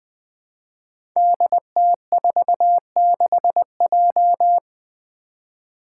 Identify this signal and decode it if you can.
Morse code, heard as a keyed tone, about 20 wpm, 700 Hz. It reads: DT46J
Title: Five Challenging Letter-Number Combinations 20wpm